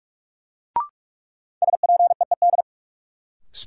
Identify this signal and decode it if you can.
Morse code, heard as a keyed tone, about 46 wpm, 700 Hz. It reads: SPEED
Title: Sentences from Top 800 Words - New Word Drill - 45wpm